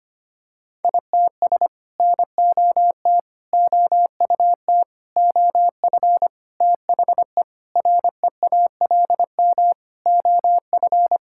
Morse code, heard as a keyed tone, about 25 wpm, 700 Hz. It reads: ITS NOT OUT OF THE REALM OF
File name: Sets of 9 Words - 25wpm